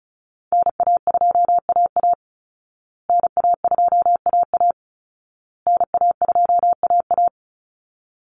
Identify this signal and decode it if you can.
Morse code, heard as a keyed tone, about 34 wpm, 700 Hz. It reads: NA2AA NA2AA NA2AA
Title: Call Signs Repeated 2x - 35wpm